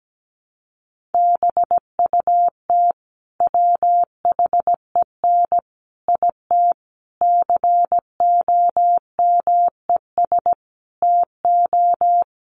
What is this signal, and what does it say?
Morse code, heard as a keyed tone, about 17 wpm, 700 Hz. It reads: BUT WHEN IT COMES TO
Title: Sets of 5 Words 17wpm